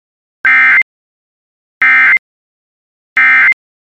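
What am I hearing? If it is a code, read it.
Emergency Alert System (SAME) tones, heard as harsh buzzing bursts. It is heard from 0.4 to 3.5 s.
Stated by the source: Emergency alert tones from a Weatherscan IntelliStar weather channel system.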